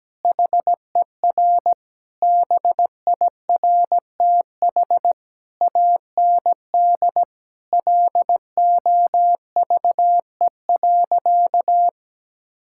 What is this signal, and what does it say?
Morse code, heard as a keyed tone, about 17 wpm, 700 Hz. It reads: HER BIRTH AND LOVE.
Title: Sentences from Top 1000 Words - Review Part 5 - 17wpm